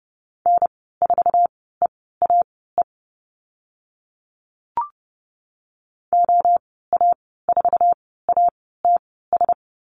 Morse code, heard as a keyed tone, about 30 wpm, spaced as for 17 wpm, 700 Hz. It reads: N4EAE OA4ATS
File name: Lesson 15g - Number 5 - Call Signs 30wpm